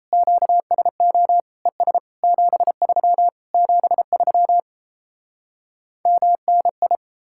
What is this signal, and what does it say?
Morse code, heard as a keyed tone, about 37 wpm, 700 Hz. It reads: QSO ES 73 73 MNI